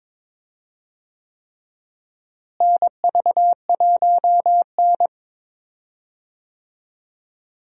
Morse code, heard as a keyed tone, about 22 wpm, 700 Hz. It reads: NV1N
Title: USA Call Signs - 4 Characters - Repeated 3x - 22wpm